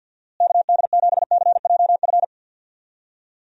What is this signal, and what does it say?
Morse code, heard as a keyed tone, about 49 wpm, 700 Hz. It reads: KD7CJF
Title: USA Call Signs - 6 Characters - Repeated 3x - 50wpm